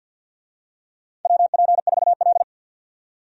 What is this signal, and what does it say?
Morse code, heard as a keyed tone, about 51 wpm, 700 Hz. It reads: WP4L
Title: USA Call Signs - 4 Characters - 50wpm